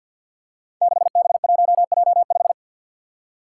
Morse code, heard as a keyed tone, about 53 wpm, 700 Hz. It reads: 6B1J5
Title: Five Challenging Letter-Number Combinations 50wpm